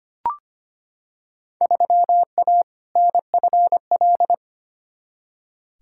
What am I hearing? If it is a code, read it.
Morse code, heard as a keyed tone, about 25 wpm, 700 Hz. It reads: 3A NFL
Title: Field Day Exchanges - 25wpm